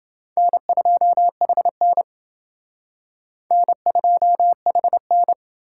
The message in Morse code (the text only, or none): N2HN N2HN